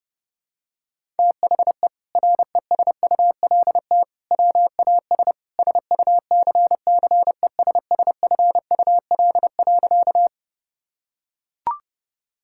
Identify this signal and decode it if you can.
Morse code, heard as a keyed tone, about 32 wpm, 700 Hz. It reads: THE RESULT WAS SUCCESSFUL.
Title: Sentences from Top 1000 Words - Review Part 6 - 30wpm